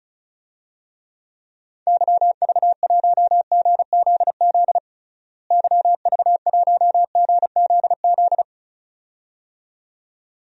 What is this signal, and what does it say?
Morse code, heard as a keyed tone, about 34 wpm, 700 Hz. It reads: YV1GZZ YV1GZZ
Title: DXCC-100 Call Signs - 6 Characters - Repeated 2x - 35wpm